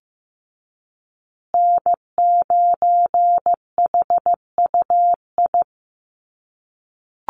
Morse code, heard as a keyed tone, about 15 wpm, 700 Hz. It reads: N9HUI
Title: USA Call Signs Repeated 3x 15wpm